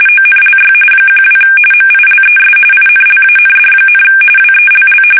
RTTY, about 45.45 baud. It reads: ONLY 12 MILES INLAND, WELL
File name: fsk850 / battleshipnj